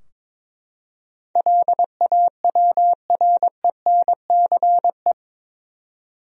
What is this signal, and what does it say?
Morse code, heard as a keyed tone, about 22 wpm, 700 Hz. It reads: LAWRENCE
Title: USA Common Names - 22wpm